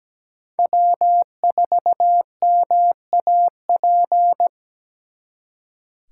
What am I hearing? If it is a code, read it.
Morse code, heard as a keyed tone, about 20 wpm, 700 Hz. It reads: W4MAP
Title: Speed Racing - USA 1x3 Call Signs 17wpm